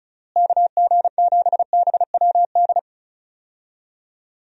Morse code, heard as a keyed tone, about 36 wpm, 700 Hz. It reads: KG7BWD
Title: USA Call Signs - 6 Characters - Repeated 3x - 35wpm